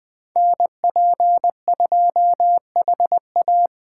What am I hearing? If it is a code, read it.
Morse code, heard as a keyed tone, about 21 wpm, 700 Hz. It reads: NP2HA